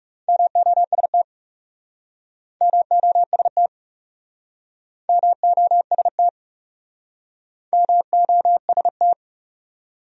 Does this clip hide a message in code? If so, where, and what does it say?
Yes: Morse code from 0.3 to 9.1 s, heard as a keyed tone, about 40 wpm, 700 Hz. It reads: MOST MOST MOST MOST